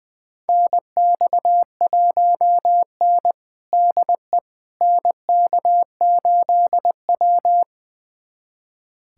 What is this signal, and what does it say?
Morse code, heard as a keyed tone, about 20 wpm, 700 Hz. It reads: NX1N DE NK8W
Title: USA Callsign DE Callsign - 4 Characters - 20wpm